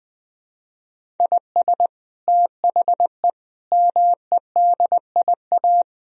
Morse code, heard as a keyed tone, about 20 wpm, 700 Hz. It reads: IS THE MEDIA
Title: Sentences from Top 1000 Words - Review Part 4 - 20wpm